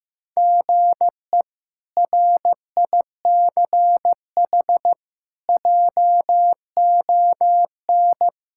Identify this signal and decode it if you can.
Morse code, heard as a keyed tone, about 16 wpm, 700 Hz. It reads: GE RICH JON
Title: K1USN SST QSO Practice - 15wpm